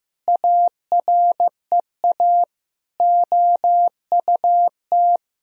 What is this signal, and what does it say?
Morse code, heard as a keyed tone, about 15 wpm, 700 Hz. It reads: AREA OUT
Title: Sentences from Top 1000 Words - Review Part 2 - 15wpm